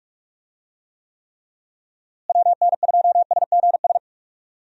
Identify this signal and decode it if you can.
Morse code, heard as a keyed tone, about 46 wpm, 700 Hz. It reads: WN2SGS
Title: USA Call Signs - 6 Characters - Repeated 3x - 45wpm